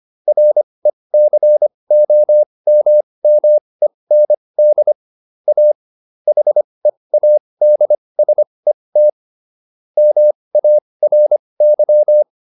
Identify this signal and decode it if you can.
Morse code, heard as a keyed tone, about 25 wpm, spaced as for 20 wpm, 600 Hz. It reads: RECOMMEND A HEADSET MARY